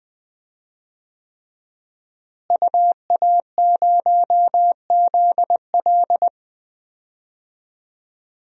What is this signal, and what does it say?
Morse code, heard as a keyed tone, about 20 wpm, 700 Hz. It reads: UA0ZL
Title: DXCC-100 Call Signs - 5 Characters - Repeated 3x - 20wpm